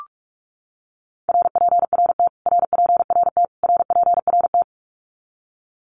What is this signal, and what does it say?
Morse code, heard as a keyed tone, about 44 wpm, 700 Hz. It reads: RPRT RPRT RPRT
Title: Abbreviations Repeated 3x - 45wpm